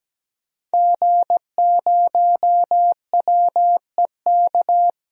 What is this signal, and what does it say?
Morse code, heard as a keyed tone, about 17 wpm, 700 Hz. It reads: G0WEK
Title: DXCC-100 Call Signs - 5 Characters - Repeated 2x - 17wpm